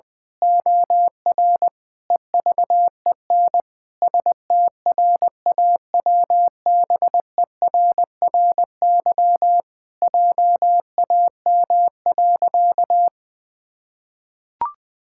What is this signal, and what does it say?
Morse code, heard as a keyed tone, about 20 wpm, 700 Hz. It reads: OR EVEN STRAWBERRY JAM.
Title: Pangram Challenge v2 - 20wpm